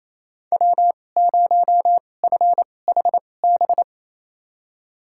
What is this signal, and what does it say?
Morse code, heard as a keyed tone, about 28 wpm, spaced as for 20 wpm, 700 Hz. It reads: W0FHB
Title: USA Call Signs - 5 Characters - 2x Character Spacing - 28wpm